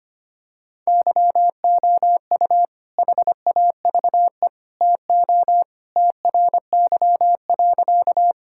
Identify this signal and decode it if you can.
Morse code, heard as a keyed tone, about 25 wpm, 700 Hz. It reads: YOU HAVE TO TRY.
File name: practice / 25wpm / Sentences with 3-Word Phrases - 25wpm